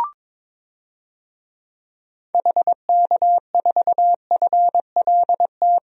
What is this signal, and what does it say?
Morse code, heard as a keyed tone, about 22 wpm, 700 Hz. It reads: HK4FLT
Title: DXCC-100 Call Signs - 6 Characters - 22wpm